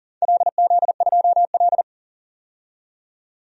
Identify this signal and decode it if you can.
Morse code, heard as a keyed tone, about 40 wpm, 700 Hz. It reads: LZ2L
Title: DXCC-100 Call Signs - 4 Characters - 40wpm